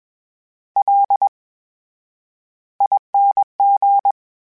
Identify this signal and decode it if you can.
Morse code, heard as a keyed tone, about 21 wpm, spaced as for 10 wpm, 800 Hz. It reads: LING
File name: Special INGs-mp3s / Special INGs-mp3s / VerySlowToFaster Ing @20